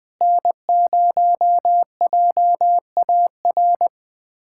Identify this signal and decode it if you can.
Morse code, heard as a keyed tone, about 20 wpm, 700 Hz. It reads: N0JAR
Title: USA Callsign DE Callsign - 5 Characters - 20wpm